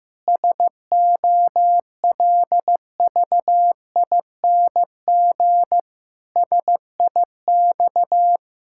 Morse code, heard as a keyed tone, about 15 wpm, 700 Hz. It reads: SOLVING SIX
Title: Pangram Challenge v2 - 15wpm